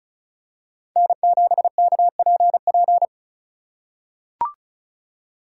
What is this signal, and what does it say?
Morse code, heard as a keyed tone, about 34 wpm, 700 Hz. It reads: N7KPP
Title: USA Call Signs - 5 Characters - 4x Character Spacing - 35wpm